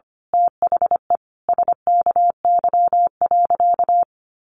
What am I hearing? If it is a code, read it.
Morse code, heard as a keyed tone, about 25 wpm, 700 Hz. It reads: THE SKY.
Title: A Sound Of Thunder 25wpm